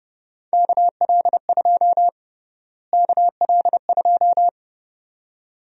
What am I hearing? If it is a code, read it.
Morse code, heard as a keyed tone, about 32 wpm, 700 Hz. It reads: KL2 KL2
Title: USA Call Sign Prefixes - Repeated 2x - 30wpm